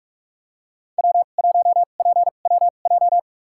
Morse code, heard as a keyed tone, about 41 wpm, spaced as for 31 wpm, 700 Hz. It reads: W1PWJ
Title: USA Call Signs - 5 Characters - 2x Character Spacing - 45wpm